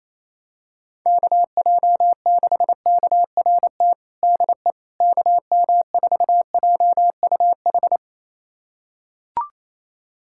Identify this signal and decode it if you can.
Morse code, heard as a keyed tone, about 28 wpm, 700 Hz. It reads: KJ6KRT DE KM4JUH
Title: USA Callsign DE Callsign 28wpm